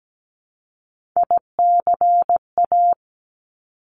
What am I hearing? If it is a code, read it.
Morse code, heard as a keyed tone, about 17 wpm, 700 Hz. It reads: ICA
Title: Top Three Letter Combinations in Words 17wpm